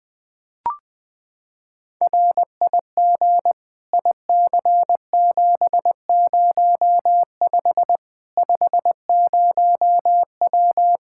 Morse code, heard as a keyed tone, about 21 wpm, 700 Hz. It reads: RIG IC705 50W
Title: Practice QSO - Exchange 2 - Info - 20wpm